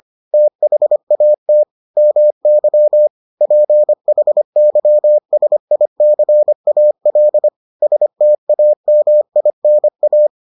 Morse code, heard as a keyed tone, about 25 wpm, 600 Hz. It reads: THAT MY PHYSICAL STAMINA